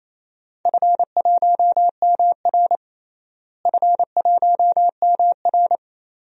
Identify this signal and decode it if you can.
Morse code, heard as a keyed tone, about 28 wpm, 700 Hz. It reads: F1MR F1MR
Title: DXCC-100 Call Signs - 4 Characters - Repeated 2x - 28wpm